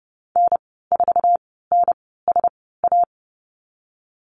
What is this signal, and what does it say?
Morse code, heard as a keyed tone, about 30 wpm, spaced as for 17 wpm, 700 Hz. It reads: N4NSA
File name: Lesson 14g - Number 2 - Call Signs 30wpm